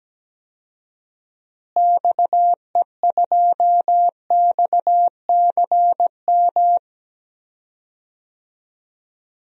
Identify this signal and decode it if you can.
Morse code, heard as a keyed tone, about 17 wpm, 700 Hz. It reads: XE2XCM
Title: DXCC-100 Call Signs - 6 Characters - Repeated 3x - 17wpm